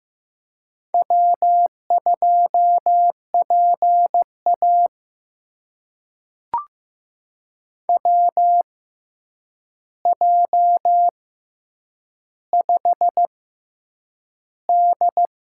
Morse code, heard as a keyed tone, about 15 wpm, spaced as for 8 wpm, 700 Hz. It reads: W2PA WJ5D
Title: USA Call Signs - 4 Characters - 6x Character Spacing - 15wpm